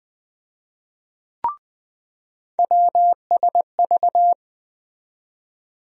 Morse code, heard as a keyed tone, about 20 wpm, 700 Hz. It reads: WSV
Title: USA Call Sign Postfixes - 3 Letters 20wpm